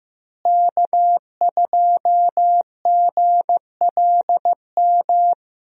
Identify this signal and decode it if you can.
Morse code, heard as a keyed tone, about 15 wpm, 700 Hz. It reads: K2GLM
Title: USA Call Signs - 5 Characters - Repeated 3x - 15wpm